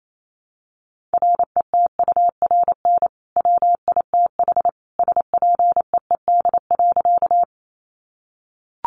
Morse code, heard as a keyed tone, about 28 wpm, 700 Hz. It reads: RETURN WITH SPEED.
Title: Sentences from Top 800 Words 28wpm